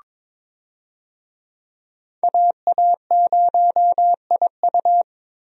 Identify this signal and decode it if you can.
Morse code, heard as a keyed tone, about 22 wpm, 700 Hz. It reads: AA0IU
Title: USA Call Signs - 5 Characters - Repeated 2x - 22wpm